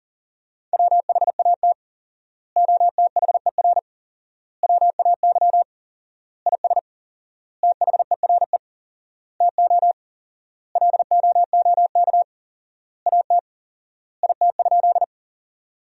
Morse code, heard as a keyed tone, about 40 wpm, 700 Hz. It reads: WHAT OTHER WAY IS THERE TO LOOK AT IT?
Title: 4x Word-Spacing - Sentences from Top 100 Words - 40wpm